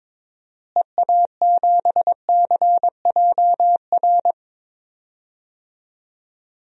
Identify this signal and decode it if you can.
Morse code, heard as a keyed tone, about 21 wpm, 700 Hz. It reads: EA7CJR
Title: DXCC-100 Call Signs - 6 Characters - Repeated 3x - 22wpm